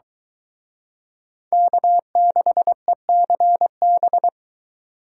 Morse code, heard as a keyed tone, about 22 wpm, 700 Hz. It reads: K6ECB